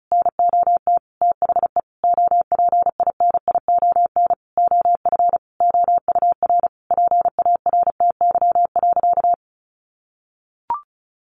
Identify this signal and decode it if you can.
Morse code, heard as a keyed tone, about 34 wpm, 700 Hz. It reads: NOT THE OPINION OF OUR PARTY.
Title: Sentences from Top 1400 Words 35wpm